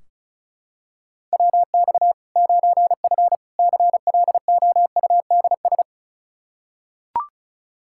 Morse code, heard as a keyed tone, about 36 wpm, 700 Hz. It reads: WX 9F CLOUDS
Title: K1USN SST QSO Practice 35wpm